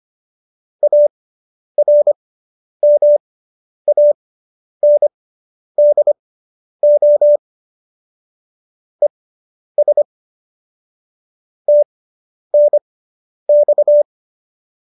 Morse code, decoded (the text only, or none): ARMANDO ES TNX